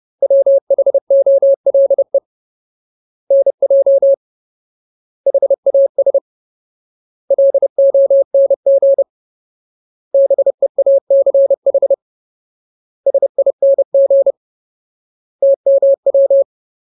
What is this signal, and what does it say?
Morse code, heard as a keyed tone, about 30 wpm, 550 Hz. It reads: WHOLE NJ HAS LONG BEACH SING TMW
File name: Ultimate Word List / 30WPM Ultimate